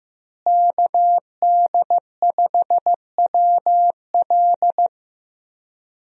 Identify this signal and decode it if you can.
Morse code, heard as a keyed tone, about 15 wpm, 700 Hz. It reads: KD5WL
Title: USA Call Signs - 5 Characters - Repeated 2x - 15wpm